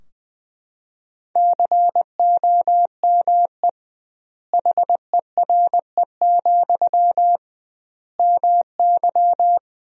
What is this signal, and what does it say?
Morse code, heard as a keyed tone, about 20 wpm, 700 Hz. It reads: COME HERE, MY
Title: 2x Word-Spacing - Sentences from Top 300 Words - 20wpm